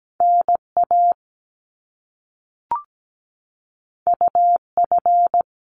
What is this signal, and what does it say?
Morse code, heard as a keyed tone, about 17 wpm, 700 Hz. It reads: NA UF
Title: Top Two Letter Combinations in Words 17wpm